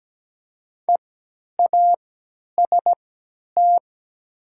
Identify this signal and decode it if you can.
Morse code, heard as a keyed tone, about 17 wpm, spaced as for 10 wpm, 700 Hz. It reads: EAST
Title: 3x Character Spacing - Top 500 Words - 17wpm